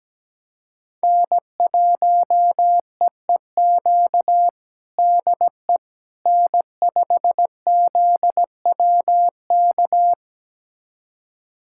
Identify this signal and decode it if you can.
Morse code, heard as a keyed tone, about 17 wpm, 700 Hz. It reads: N1EEQ DE N5ZWK
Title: USA Callsign DE Callsign - 5 Characters - 17wpm